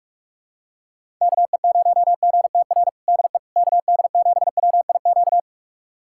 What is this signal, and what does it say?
Morse code, heard as a keyed tone, about 46 wpm, 700 Hz. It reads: KE0GTR DE KD7WIQ
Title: USA Callsign DE Callsign - 6 Characters - 45wpm